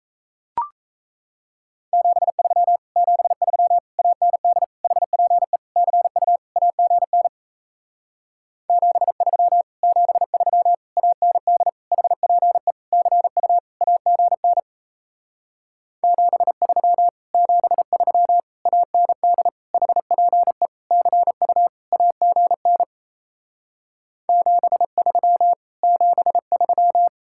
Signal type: Morse code

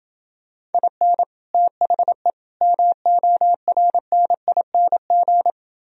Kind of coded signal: Morse code

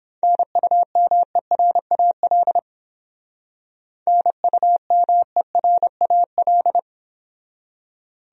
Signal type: Morse code

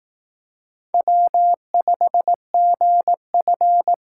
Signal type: Morse code

